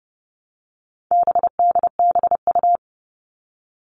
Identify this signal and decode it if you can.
Morse code, heard as a keyed tone, about 30 wpm, 700 Hz. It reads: BDBU
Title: All Cut Numbers - 4-Digit 30wpm